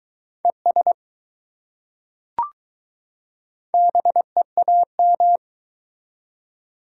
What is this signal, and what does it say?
Morse code, heard as a keyed tone, about 21 wpm, 700 Hz. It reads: ES BEAM